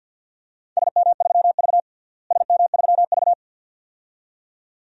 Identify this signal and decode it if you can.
Morse code, heard as a keyed tone, about 48 wpm, 700 Hz. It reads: SM3V SM3V
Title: DXCC-100 Call Signs - 4 Characters - Repeated 2x - 50wpm